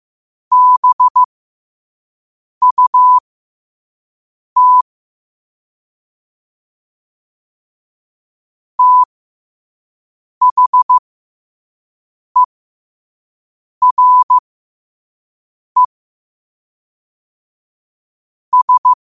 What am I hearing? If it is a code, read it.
Morse code, heard as a keyed tone, about 15 wpm, spaced as for 5 wpm, 1000 Hz. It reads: BUT THERE S
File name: qotdfiles / 5wpm